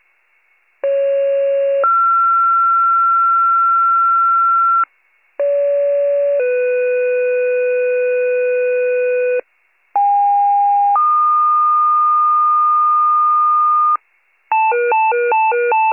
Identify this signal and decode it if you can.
Two-tone page, heard as a one-second tone, then a three-second tone, about 550 then 1360 Hz, about 570 then 500 Hz, about 810 then 1220 Hz, with noise. It is heard from 0.8 to 14.0 s.